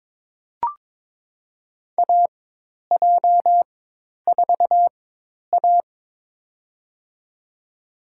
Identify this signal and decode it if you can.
Morse code, heard as a keyed tone, about 22 wpm, spaced as for 10 wpm, 700 Hz. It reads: AJ4A
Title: USA Call Signs - 4 Characters - 4x Character Spacing - 22wpm